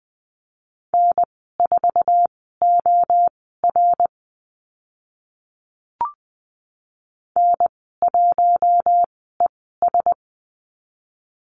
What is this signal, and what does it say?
Morse code, heard as a keyed tone, about 20 wpm, spaced as for 14 wpm, 700 Hz. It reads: N4OR N1ES
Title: 069 Lesson 12g - Letter d - Call Signs 20wpm